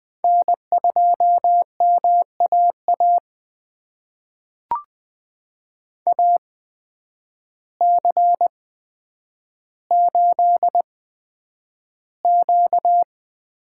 Morse code, heard as a keyed tone, about 20 wpm, spaced as for 9 wpm, 700 Hz. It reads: N2MAA AC8Q
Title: USA Call Signs - 8x Character Spacing - 20wpm